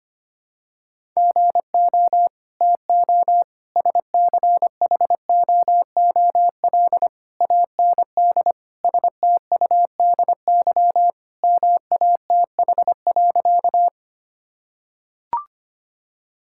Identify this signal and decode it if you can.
Morse code, heard as a keyed tone, about 25 wpm, 700 Hz. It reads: GO TO SCHOOL AND STUDY MATH.
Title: Sentences from Top 1000 Words - Review Part 6 - 25wpm